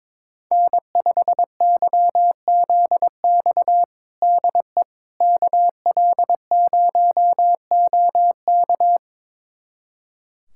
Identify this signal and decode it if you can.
Morse code, heard as a keyed tone, about 21 wpm, 700 Hz. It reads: N5YZX DE KL0OK
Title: USA Callsign DE Callsign - 5 Characters - 22wpm